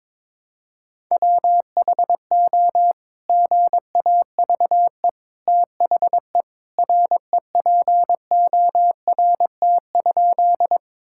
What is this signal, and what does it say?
Morse code, heard as a keyed tone, about 22 wpm, 700 Hz. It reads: WHO GAVE THE REPORT?
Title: Sentences from Top 1000 Words - Review Part 6 - 22wpm